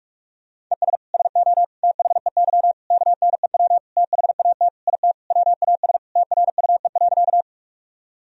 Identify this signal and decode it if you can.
Morse code, heard as a keyed tone, about 46 wpm, 700 Hz. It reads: ES SO THEY KNEW THAT IT WAS TRUE.
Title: Sentences from Top 400 Words and Abbreviations 45wpm